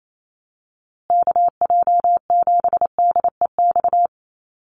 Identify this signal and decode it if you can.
Morse code, heard as a keyed tone, about 28 wpm, 700 Hz. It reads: KJ7DEX
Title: USA Call Signs Repeated 3x 28wpm